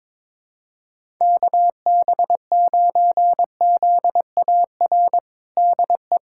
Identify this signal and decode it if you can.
Morse code, heard as a keyed tone, about 21 wpm, 700 Hz. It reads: KB9ZAR DE
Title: USA Callsign DE Callsign - 6 Characters - 22wpm